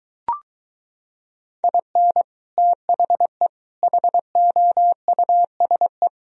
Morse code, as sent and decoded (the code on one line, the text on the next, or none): .. -. / - .... / .... --- ..- ... .
IN THE HOUSE